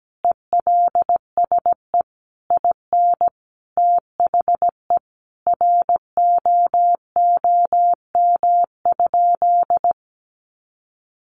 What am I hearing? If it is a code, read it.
Morse code, heard as a keyed tone, about 17 wpm, 700 Hz. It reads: ELSE IN THE ROOM?
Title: Sentences from Top 900 Words 17wpm